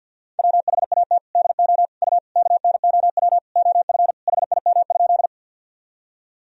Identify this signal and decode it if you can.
Morse code, heard as a keyed tone, about 53 wpm, 700 Hz. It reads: WHAT DO U KNOW OF HIM?
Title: Sentences from Top 100 Words and Abbreviations 50wpm